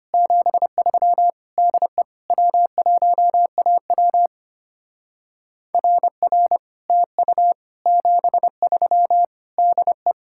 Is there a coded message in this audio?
Yes: Morse code from 0.1 to 10.1 s, heard as a keyed tone, about 32 wpm, 700 Hz. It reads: 73 DE W1AW RR TU 73 DE